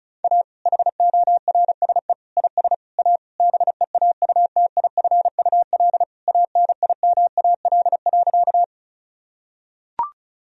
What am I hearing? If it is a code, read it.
Morse code, heard as a keyed tone, about 36 wpm, 700 Hz. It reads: A HORSE IS A BEAUTIFUL ANIMAL.
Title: Sentences from Top 1000 Words - Review Part 1 - 35wpm